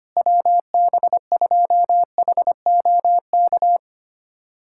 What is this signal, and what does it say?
Morse code, heard as a keyed tone, about 25 wpm, 700 Hz. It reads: WB2HOK
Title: USA Callsign DE Callsign - 6 Characters - 25wpm